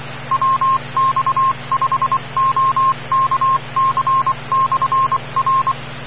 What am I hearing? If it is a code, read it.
Morse code, heard as a keyed tone, about 25 wpm, 1060 Hz, with noise. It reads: WX5OKC/R
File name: RR Skywarn / WX5OKC Amateur Repeater Hz